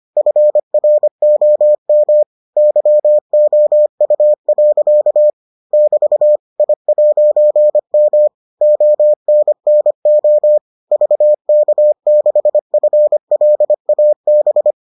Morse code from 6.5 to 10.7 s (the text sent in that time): I'M ONNO